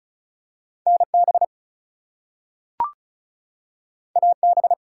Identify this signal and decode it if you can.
Morse code, heard as a keyed tone, about 36 wpm, 700 Hz. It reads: NB AB
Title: US States and Canadian Provinces 35wpm